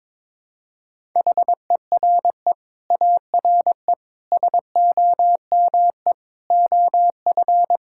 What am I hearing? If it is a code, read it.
Morse code, heard as a keyed tone, about 22 wpm, 700 Hz. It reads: HERE ARE SOME OF
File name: Sets of 9 Words - 22wpm